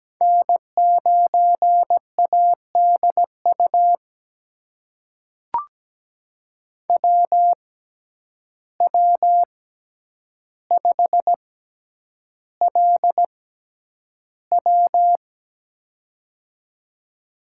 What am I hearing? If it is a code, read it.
Morse code, heard as a keyed tone, about 17 wpm, spaced as for 9 wpm, 700 Hz. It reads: N9ADU WW5LW
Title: USA Call Signs - 5 Characters - 6x Character Spacing - 17wpm